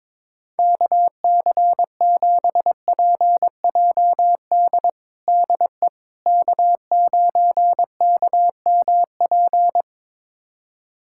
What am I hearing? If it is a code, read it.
Morse code, heard as a keyed tone, about 22 wpm, 700 Hz. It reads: KC7PJD DE K9KMP
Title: USA Callsign DE Callsign 22wpm